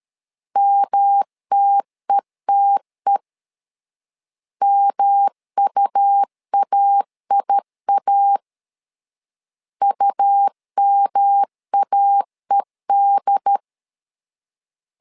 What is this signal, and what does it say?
Morse code, heard as a keyed tone, about 13 wpm, 800 Hz. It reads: MTETE MUAIA UMAED
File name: LEZIONI / 3 MWRO / 3 M 12wpm 1541